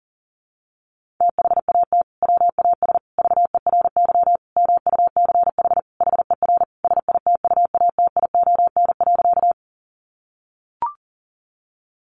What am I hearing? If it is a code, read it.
Morse code, heard as a keyed tone, about 37 wpm, 700 Hz. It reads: THAT WAS VERY MUCH HER SITUATION.
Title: Sentences from Top 1200 Words 40wpm